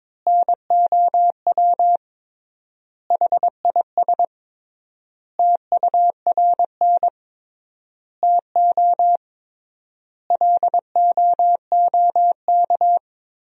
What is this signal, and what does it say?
Morse code, heard as a keyed tone, about 22 wpm, 700 Hz. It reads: NOW HIS TURN TO LOOK
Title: 3x Word-Spacing - Sentences from Top 200 Words - 22wpm